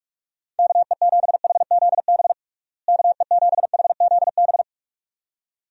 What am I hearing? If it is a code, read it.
Morse code, heard as a keyed tone, about 43 wpm, 700 Hz. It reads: KE7HZB KE7HZB
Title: USA Call Signs - 6 Characters - Repeated 2x - 45wpm